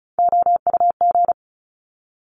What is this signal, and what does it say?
Morse code, heard as a keyed tone, about 36 wpm, 700 Hz. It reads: OUG
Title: Top Three Letter Combinations in Words 35wpm